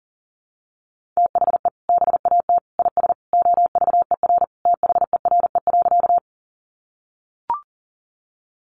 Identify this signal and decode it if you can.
Morse code, heard as a keyed tone, about 40 wpm, 700 Hz. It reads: THE BAT IS OVER THERE.
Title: Sentences from Top 1500 Words 40wpm